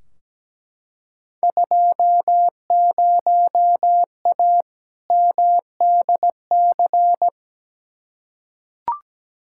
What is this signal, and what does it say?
Morse code, heard as a keyed tone, about 17 wpm, 700 Hz. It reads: 20A MDC
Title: Field Day Exchanges Encore - 17wpm